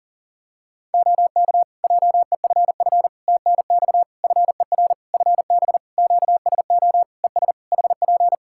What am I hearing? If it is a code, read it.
Morse code, heard as a keyed tone, about 40 wpm, 700 Hz. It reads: OK JEFF TNX FER FB QSO ES HP
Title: K1USN SST QSO Practice 40wpm